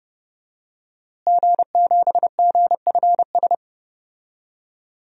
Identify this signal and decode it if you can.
Morse code, heard as a keyed tone, about 30 wpm, 700 Hz. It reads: G7GFS
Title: DXCC-100 Call Signs - 5 Characters - Repeated 2x - 30wpm